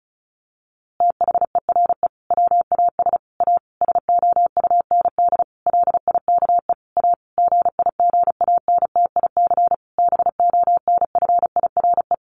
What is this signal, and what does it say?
Morse code, heard as a keyed tone, about 34 wpm, 700 Hz. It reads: THERE WAS A SOUND LIKE A GIGANTIC BONFIRE
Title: A Sound Of Thunder 35wpm